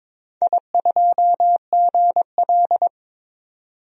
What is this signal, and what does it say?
Morse code, heard as a keyed tone, about 22 wpm, 700 Hz. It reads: I2GL
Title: DXCC-100 Call Signs - 4 Characters - 22wpm